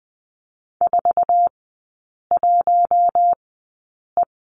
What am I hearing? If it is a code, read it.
Morse code, heard as a keyed tone, about 20 wpm, spaced as for 8 wpm, 700 Hz. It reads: 41E
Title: 047 Lesson 9e - Number 4 - Three Characters 20wpm